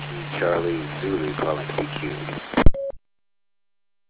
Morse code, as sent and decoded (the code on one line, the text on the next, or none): none